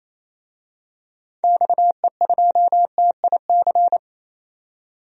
Morse code, heard as a keyed tone, about 28 wpm, 700 Hz. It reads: XE2TIC
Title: DXCC-100 Call Signs - 6 Characters - 28wpm